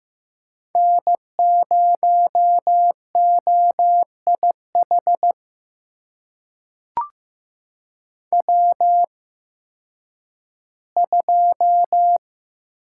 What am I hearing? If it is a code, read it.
Morse code, heard as a keyed tone, about 15 wpm, spaced as for 10 wpm, 700 Hz. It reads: N0OIH W2
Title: USA Call Signs - 5 Characters - 8x Character Spacing - 15wpm